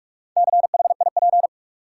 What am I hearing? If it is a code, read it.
Morse code, heard as a keyed tone, about 46 wpm, 700 Hz. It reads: CHIP
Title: Four Letter Words - 45wpm